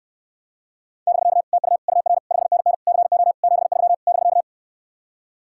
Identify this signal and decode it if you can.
Morse code, heard as a keyed tone, about 69 wpm, 700 Hz. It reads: <BK> TU UR 5NN CO 73 <BK>